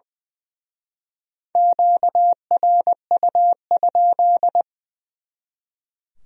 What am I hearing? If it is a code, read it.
Morse code, heard as a keyed tone, about 24 wpm, 700 Hz. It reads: QRU?
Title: Speed-Racing Q-Codes - 20wpm